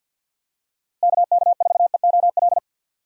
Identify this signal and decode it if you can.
Morse code, heard as a keyed tone, about 51 wpm, 700 Hz. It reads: KK4EOL
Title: USA Call Signs - 6 Characters - Repeated 3x - 50wpm